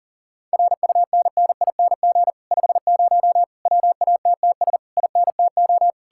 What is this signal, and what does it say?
Morse code, heard as a keyed tone, about 52 wpm, 700 Hz. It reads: RUNNING 50 WATTS INTO